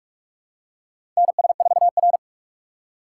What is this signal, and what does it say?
Morse code, heard as a keyed tone, about 44 wpm, 690 Hz. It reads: NS4R
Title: USA Call Signs - 4 Characters - 3x Character Spacing - 45wpm